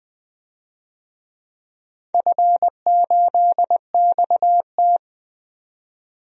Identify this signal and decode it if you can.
Morse code, heard as a keyed tone, about 21 wpm, 700 Hz. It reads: F8XT
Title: DXCC-100 Call Signs - 4 Characters - 20wpm